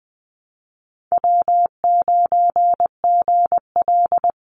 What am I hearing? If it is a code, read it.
Morse code, heard as a keyed tone, about 20 wpm, 700 Hz. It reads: W9GL
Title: Call Signs - 20wpm